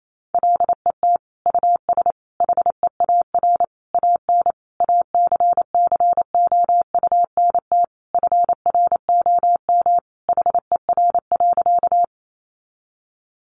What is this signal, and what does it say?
Morse code, heard as a keyed tone, about 28 wpm, 700 Hz. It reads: LET US HEAR AN ACCOUNT FROM HER.
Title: Sentences from Top 900 Words 28wpm